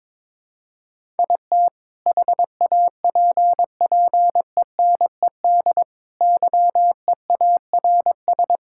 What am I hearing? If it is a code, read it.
Morse code, heard as a keyed tone, about 22 wpm, 700 Hz. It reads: IT HAPPENED YEARS
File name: practice / 22wpm / Sentences with 2-Word Phrases - 22wpm